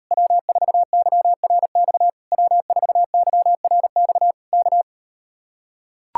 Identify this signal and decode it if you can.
Morse code, heard as a keyed tone, about 38 wpm, 700 Hz. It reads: W4YRX W4YRX K